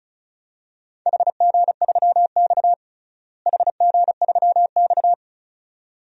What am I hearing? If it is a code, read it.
Morse code, heard as a keyed tone, about 34 wpm, 700 Hz. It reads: HG3X HG3X
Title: DXCC-100 Call Signs - 4 Characters - Repeated 2x - 35wpm